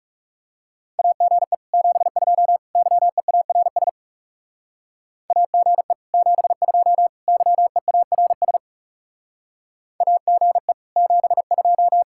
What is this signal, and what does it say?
Morse code, heard as a keyed tone, about 38 wpm, 700 Hz. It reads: AGE 72 YEARS AGE 72 YEARS AGE 72